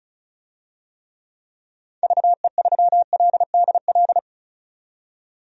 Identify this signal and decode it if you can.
Morse code, heard as a keyed tone, about 34 wpm, 700 Hz. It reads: VE3LDL
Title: DXCC-100 Call Signs - 6 Characters - 35wpm